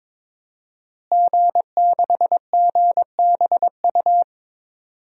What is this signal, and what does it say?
Morse code, heard as a keyed tone, about 22 wpm, 700 Hz. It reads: G6GBU
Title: DXCC-100 Call Signs - 5 Characters - Repeated 3x - 22wpm